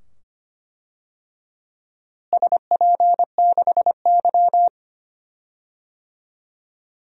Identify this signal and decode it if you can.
Morse code, heard as a keyed tone, about 25 wpm, 700 Hz. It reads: SP6Y
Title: DXCC-100 Call Signs - 4 Characters - Repeated 2x - 25wpm